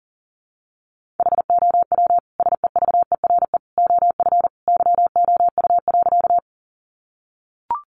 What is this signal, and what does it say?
Morse code, heard as a keyed tone, about 37 wpm, 700 Hz. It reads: HOW SEVERE OF YOU.
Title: Sentences from Top 1400 Words 40wpm